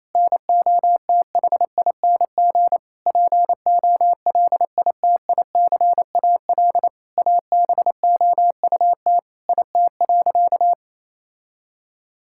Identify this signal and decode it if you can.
Morse code, heard as a keyed tone, about 28 wpm, 700 Hz. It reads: NOTHING POLITICAL ABOUT IT.
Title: Sentences from Top 1000 Words - Review Part 5 - 28wpm